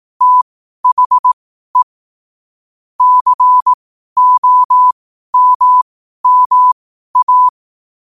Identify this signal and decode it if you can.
Morse code, heard as a keyed tone, about 18 wpm, spaced as for 12 wpm, 1000 Hz. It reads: THE COMMA
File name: qotdfiles / 12wpm